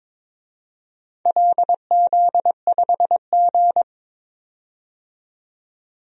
Morse code, heard as a keyed tone, about 22 wpm, 700 Hz. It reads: LZ5G
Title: DXCC-100 Call Signs - 4 Characters - 22wpm